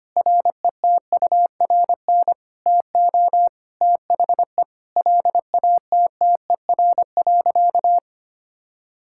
Morse code, heard as a keyed tone, about 25 wpm, 700 Hz. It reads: RETURN TO THE LATTER.
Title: Sentences from Top 1600 Words 25wpm